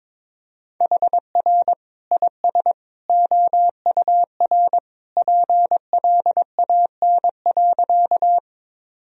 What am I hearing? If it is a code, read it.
Morse code, heard as a keyed tone, about 22 wpm, 700 Hz. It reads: HR IS OUR PLAN.
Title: Sentences from Top 500 Words and Abbreviations 22wpm